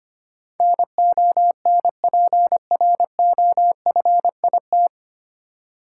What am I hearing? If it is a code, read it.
Morse code, heard as a keyed tone, about 25 wpm, 700 Hz. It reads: NONPROFIT
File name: Nine Letter Words - 25wpm